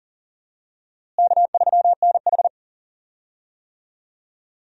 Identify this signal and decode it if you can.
Morse code, heard as a keyed tone, about 40 wpm, 700 Hz. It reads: K3NH
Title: USA Call Signs - 4 Characters - Repeated 3x - 40wpm